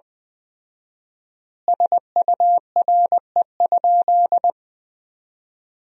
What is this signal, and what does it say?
Morse code, heard as a keyed tone, about 21 wpm, 700 Hz. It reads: SURE?
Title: 4x Word-Spacing - Sentences from Top 300 Words - 20wpm